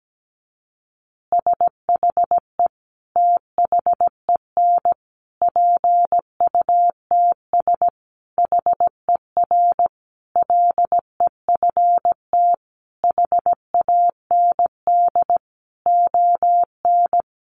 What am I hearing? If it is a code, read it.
Morse code, heard as a keyed tone, about 17 wpm, 700 Hz. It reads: SHE THEN PUTS HER LEFT HAND ON
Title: Easy Phrases 17wpm